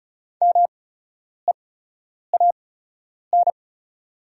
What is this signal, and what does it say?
Morse code, heard as a keyed tone, about 33 wpm, spaced as for 9 wpm, 690 Hz. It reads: MEAN
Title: 8x Character Spacing - Top 200 Words - 35wpm